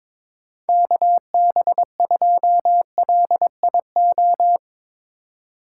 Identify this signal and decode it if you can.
Morse code, heard as a keyed tone, about 22 wpm, 700 Hz. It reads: KB2LIO
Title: USA Call Signs - 6 Characters - 3x Character Spacing - 22wpm